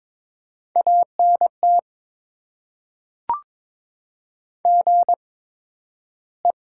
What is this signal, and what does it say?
Morse code, heard as a keyed tone, about 22 wpm, spaced as for 12 wpm, 700 Hz. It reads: ANT GE
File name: Top 20 QSO Elements - 8x Character Spacing - 22wpm